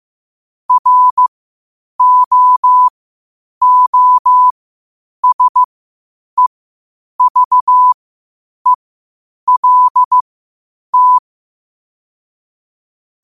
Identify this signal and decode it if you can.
Morse code, heard as a keyed tone, about 15 wpm, spaced as for 8 wpm, 1000 Hz. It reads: ROOSEVELT
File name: qotdfiles / 8wpm